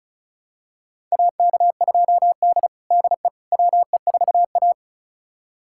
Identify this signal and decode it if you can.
Morse code, heard as a keyed tone, about 36 wpm, 690 Hz. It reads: AK2D DE WE4A
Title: USA Callsign DE Callsign - 4 Characters - 35wpm